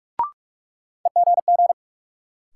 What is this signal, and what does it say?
Morse code, heard as a keyed tone, about 43 wpm, 700 Hz. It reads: EGG
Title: Sentences from Top 1000 Words - Review Part 7 - New Word Drill - 45wpm